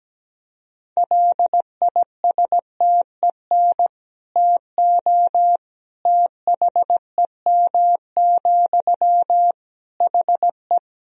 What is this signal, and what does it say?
Morse code, heard as a keyed tone, about 17 wpm, 700 Hz. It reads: LISTEN TO THEM, HE
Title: Sentences from Top 400 Words and Abbreviations 17wpm